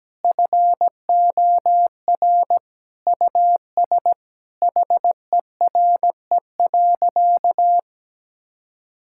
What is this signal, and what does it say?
Morse code, heard as a keyed tone, about 17 wpm, 700 Hz. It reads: FOR US HERE.
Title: Sentences from Top 1000 Words - Review Part 5 - 17wpm